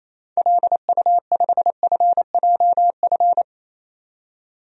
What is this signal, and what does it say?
Morse code, heard as a keyed tone, about 28 wpm, 700 Hz. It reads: LU5FJF
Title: DXCC-100 Call Signs - 6 Characters - 28wpm